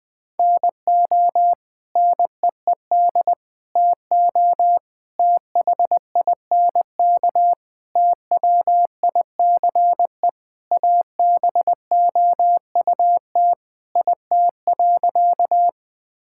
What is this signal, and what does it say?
Morse code, heard as a keyed tone, about 20 wpm, 700 Hz. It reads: NO NEED TO THINK TWICE ABOUT IT.
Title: Sentences from Top 1600 Words 20wpm